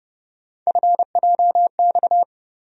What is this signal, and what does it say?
Morse code, heard as a keyed tone, about 30 wpm, 700 Hz. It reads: FJX
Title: USA Call Sign Postfixes - 3 Letters 30wpm